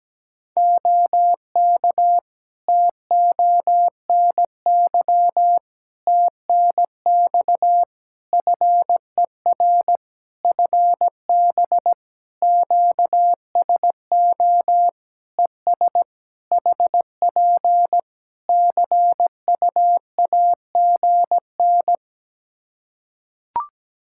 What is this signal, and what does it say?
Morse code, heard as a keyed tone, about 17 wpm, 700 Hz. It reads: OK TONY TNX FER FB QSO ES HP CUAGN
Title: Practice QSO - Exchange 4 - Ending - 17wpm